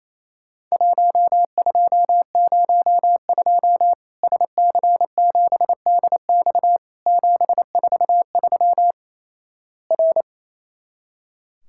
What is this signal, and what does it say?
Morse code, heard as a keyed tone, about 28 wpm, 700 Hz. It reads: 1202 SC7DX 743 R
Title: Worked All Europe DX Contest - QTC Exchange - 28wpm